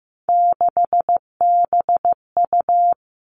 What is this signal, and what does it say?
Morse code, heard as a keyed tone, about 15 wpm, 700 Hz. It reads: 6BU
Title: All Cut Numbers - 3-Digit 15wpm